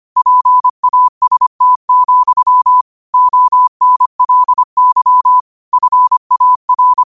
Morse code, heard as a keyed tone, about 25 wpm, 1000 Hz. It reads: PAST, ONLY FAR
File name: qotdfiles / 25wpm